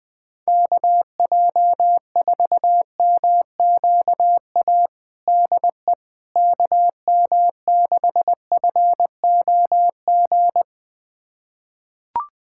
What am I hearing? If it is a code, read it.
Morse code, heard as a keyed tone, about 21 wpm, 700 Hz. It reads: KJ4MQA DE KM6FOG
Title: USA Callsign DE Callsign - 6 Characters - 20wpm